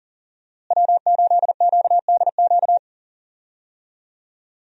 Morse code, heard as a keyed tone, about 37 wpm, 700 Hz. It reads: W8QDQ
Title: USA Call Signs - 5 Characters - Repeated 2x - 40wpm